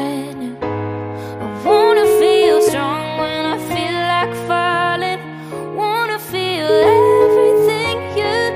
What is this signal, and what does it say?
Ringback tone (Europe, 425 Hz), with noise.